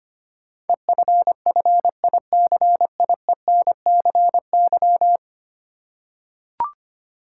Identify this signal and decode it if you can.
Morse code, heard as a keyed tone, about 25 wpm, 690 Hz. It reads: EFFICIENCY
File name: 10 Letter Words - First 3 Letters Given - 25wpm